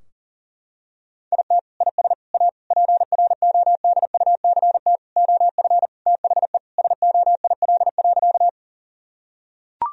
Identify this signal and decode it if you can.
Morse code, heard as a keyed tone, about 40 wpm, 700 Hz. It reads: IT IS A PRODUCT OF THE SOIL.
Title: Sentences from Top 1000 Words - Review Part 4 - 40wpm